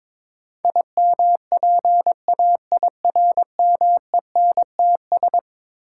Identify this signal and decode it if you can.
Morse code, heard as a keyed tone, about 22 wpm, 700 Hz. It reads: IMPAIRMENTS
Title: Sets of 5 Words Encore - New Word Drill - 22wpm